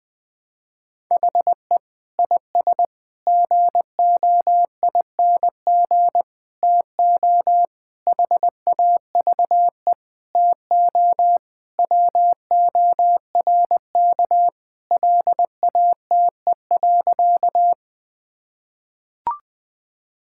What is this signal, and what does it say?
Morse code, heard as a keyed tone, about 19 wpm, 700 Hz. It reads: HE IS GOING TO HAVE TO WORK LATE.
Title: Sentences with 5-Word Phrases - 20wpm